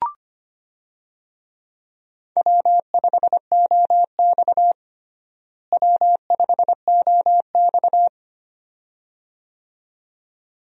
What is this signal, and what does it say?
Morse code, heard as a keyed tone, about 25 wpm, 700 Hz. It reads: W5OX W5OX
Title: USA Call Signs - 4 Characters - Repeated 2x - 25wpm